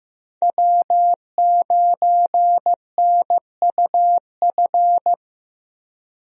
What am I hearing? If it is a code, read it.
Morse code, heard as a keyed tone, about 15 wpm, 700 Hz. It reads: W9NUF
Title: USA Call Signs - 5 Characters - Repeated 3x - 15wpm